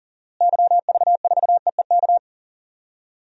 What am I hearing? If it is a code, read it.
Morse code, heard as a keyed tone, about 40 wpm, 700 Hz. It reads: YV4EEK
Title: DXCC-100 Call Signs - 6 Characters - Repeated 2x - 40wpm